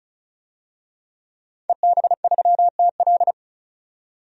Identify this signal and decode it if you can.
Morse code, heard as a keyed tone, about 34 wpm, 700 Hz. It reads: EB3TL
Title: DXCC-100 Call Signs - 5 Characters - 35wpm